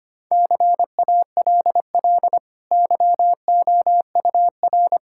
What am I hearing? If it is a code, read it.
Morse code, heard as a keyed tone, about 25 wpm, 700 Hz. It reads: CALL YOUR
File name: Easy Sentences with Hints - 25wpm